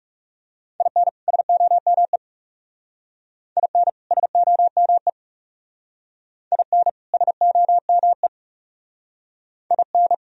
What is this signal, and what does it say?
Morse code, heard as a keyed tone, about 38 wpm, 700 Hz. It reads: IN SOME IN SOME IN SOME IN
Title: Speed-Racing - Sets of 2 Words 30wpm